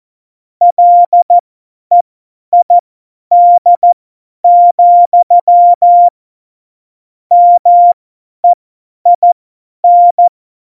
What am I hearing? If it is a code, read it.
Morse code, heard as a keyed tone, about 14 wpm, spaced as for 10 wpm, 700 Hz. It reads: LEID, MEIN
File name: Morsekurs / Witze / Witz 23. Juli 2019, 50 F70